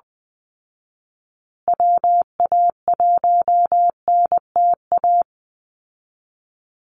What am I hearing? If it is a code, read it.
Morse code, heard as a keyed tone, about 20 wpm, 700 Hz. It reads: WA1NTA
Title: Call Signs Repeated 2x - 20wpm